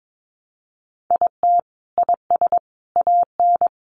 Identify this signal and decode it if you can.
Morse code, heard as a keyed tone, about 22 wpm, 700 Hz. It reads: IT IS AN
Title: Sentences from Top 1000 Words 22wpm